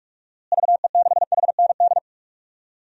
Morse code, heard as a keyed tone, about 44 wpm, 700 Hz. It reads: VE6HND
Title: DXCC-100 Call Signs - 6 Characters - Repeated 2x - 45wpm